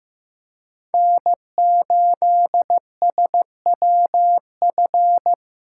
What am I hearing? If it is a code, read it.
Morse code, heard as a keyed tone, about 15 wpm, 700 Hz. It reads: N8SWF
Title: USA Call Signs - 5 Characters - 2x Character Spacing - 15wpm